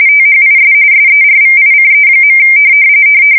RTTY, about 45.45 baud. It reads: THIS SERVICE.
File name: fsk170 / printing-telegraph